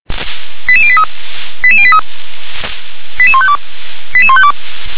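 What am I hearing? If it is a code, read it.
Five-tone selective call (ZVEI), heard as quick runs of five tones, tones of about 70 ms. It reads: 88083 88083 88152 88152
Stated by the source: ZVEI five-tone call received by SDR.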